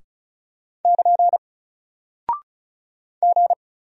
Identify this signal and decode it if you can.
Morse code, heard as a keyed tone, about 36 wpm, 700 Hz. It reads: <KN> G
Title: Single Character (Letter:Number:Punctuation:Prosign) Alt 35wpm